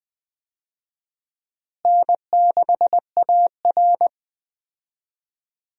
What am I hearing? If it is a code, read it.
Morse code, heard as a keyed tone, about 20 wpm, 700 Hz. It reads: N6AR
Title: USA Call Signs - 4 Characters - Repeated 2x - 20wpm